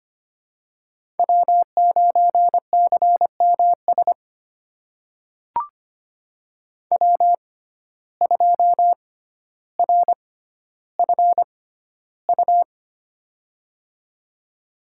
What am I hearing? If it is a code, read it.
Morse code, heard as a keyed tone, about 25 wpm, spaced as for 13 wpm, 700 Hz. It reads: W9CMS W2RFU
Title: USA Call Signs - 5 Characters - 6x Character Spacing - 25wpm